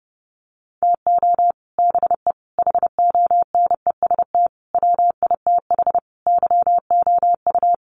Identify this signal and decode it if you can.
Morse code, heard as a keyed tone, about 30 wpm, 700 Hz. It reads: TO BE HONEST WITH YOU
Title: Sets of 5 Words Encore 30wpm